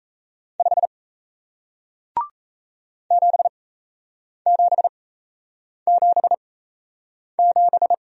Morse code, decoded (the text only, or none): 5 7777